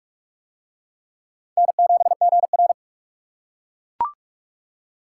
Morse code, heard as a keyed tone, about 40 wpm, 700 Hz. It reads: N7GR